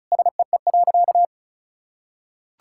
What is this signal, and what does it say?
Morse code, heard as a keyed tone, about 36 wpm, 700 Hz. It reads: SEE.
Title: Sentences from Top 1000 Words - Review Part 5 - 35wpm